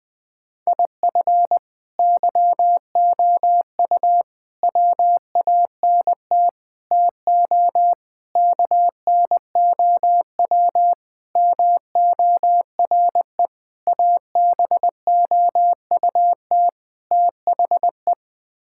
Morse code, heard as a keyed tone, about 21 wpm, 700 Hz. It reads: IF YOU WANT TO KNOW MORE ABOUT THE
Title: Sets of 8 Words - 20wpm